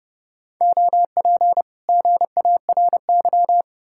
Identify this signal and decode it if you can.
Morse code, heard as a keyed tone, about 40 wpm, 700 Hz. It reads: OP GARY